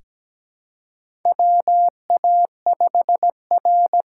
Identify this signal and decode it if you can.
Morse code, heard as a keyed tone, about 17 wpm, 700 Hz. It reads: WA5R
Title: USA Call Signs - 6 Characters - 6x Character Spacing - 17wpm